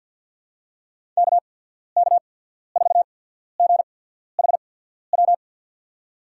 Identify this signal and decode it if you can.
Morse code, heard as a keyed tone, about 52 wpm, spaced as for 14 wpm, 700 Hz. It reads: KK4GHW
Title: USA Call Signs - 6 Characters - 8x Character Spacing - 50wpm